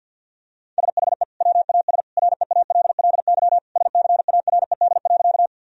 Morse code, heard as a keyed tone, about 50 wpm, 700 Hz. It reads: SHE WAS REALLY SCARED.
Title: Sentences from Top 1000 Words - Review Part 4 - 50wpm